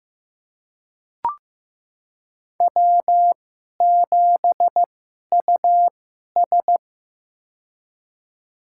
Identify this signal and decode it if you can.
Morse code, heard as a keyed tone, about 15 wpm, spaced as for 11 wpm, 700 Hz. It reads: W7US
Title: USA Call Signs - 4 Characters - 2x Character Spacing - 15wpm